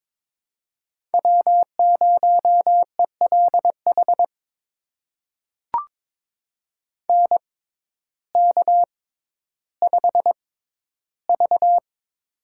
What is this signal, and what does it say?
Morse code, heard as a keyed tone, about 22 wpm, spaced as for 12 wpm, 700 Hz. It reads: W0ELH NK5V
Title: USA Call Signs - 6x Character Spacing - 22wpm